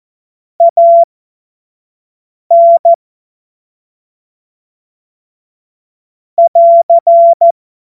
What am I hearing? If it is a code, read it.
Morse code, heard as a keyed tone, about 14 wpm, spaced as for 5 wpm, 680 Hz. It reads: AN <AR>
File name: Morsekurs / Witze / Witz 20. Juli 2021, 25 F70